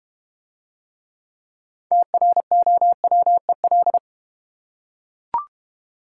Morse code, heard as a keyed tone, about 30 wpm, 700 Hz. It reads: TROWEL
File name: practice / 25wpm / Difficult Words - Speed-Racing with Hints - 25wpm